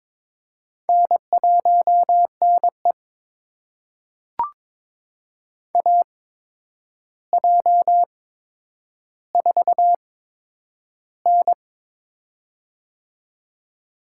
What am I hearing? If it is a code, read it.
Morse code, heard as a keyed tone, about 22 wpm, spaced as for 9 wpm, 700 Hz. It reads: N1NE AJ4N
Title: USA Call Signs - 4 Characters - 8x Character Spacing - 22wpm